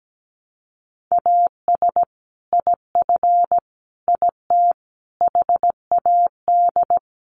Morse code, heard as a keyed tone, about 17 wpm, 700 Hz. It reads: AS IF IT HAD
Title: Sets of 5 Words Encore2 17wpm